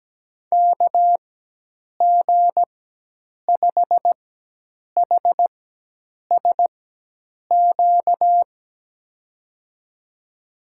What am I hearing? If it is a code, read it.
Morse code, heard as a keyed tone, about 17 wpm, spaced as for 8 wpm, 700 Hz. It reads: KG5HSQ
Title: USA Call Signs - 6 Characters - 4x Character Spacing - 17wpm